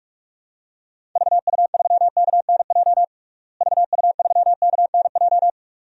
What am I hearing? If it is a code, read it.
Morse code, heard as a keyed tone, about 46 wpm, 700 Hz. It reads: VU3KNJ VU3KNJ
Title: DXCC-100 Call Signs - 6 Characters - Repeated 2x - 45wpm